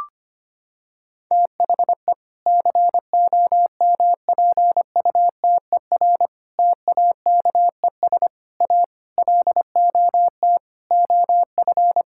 Morse code, heard as a keyed tone, about 25 wpm, 700 Hz. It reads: THE COMPUTER TAKES A LOT OF
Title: Sentences from Top 1000 Words - Review Part 6 - 25wpm